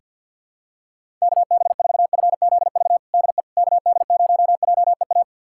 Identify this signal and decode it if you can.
Morse code, heard as a keyed tone, about 46 wpm, 700 Hz. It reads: KB4FZV DE KD0JEA
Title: USA Callsign DE Callsign - 6 Characters - 50wpm